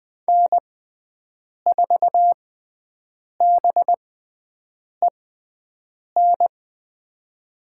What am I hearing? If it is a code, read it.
Morse code, heard as a keyed tone, about 20 wpm, spaced as for 7 wpm, 700 Hz. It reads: N4BEN
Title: USA Call Signs - 6x Character Spacing - 20wpm